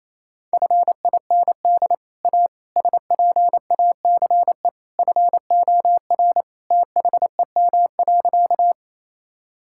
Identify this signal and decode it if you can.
Morse code, heard as a keyed tone, about 28 wpm, 700 Hz. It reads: FIND A SPACE FOR THEM.
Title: Sentences from Top 1000 Words - Review Part 5 - 28wpm